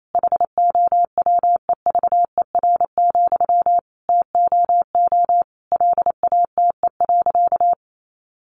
Morse code, heard as a keyed tone, about 28 wpm, 700 Hz. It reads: HOWEVER, TOO LATE.
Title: Sentences from Top 600 Words 28wpm